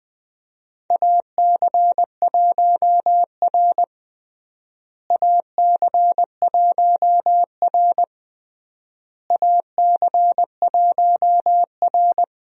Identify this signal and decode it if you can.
Morse code, heard as a keyed tone, about 21 wpm, 690 Hz. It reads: AC1R AC1R AC1R
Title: USA Call Signs - 4 Characters - Repeated 3x - 20wpm